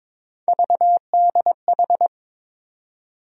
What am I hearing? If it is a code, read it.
Morse code, heard as a keyed tone, about 22 wpm, 700 Hz. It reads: VDH
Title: USA Call Sign Postfixes - 3 Letters 22wpm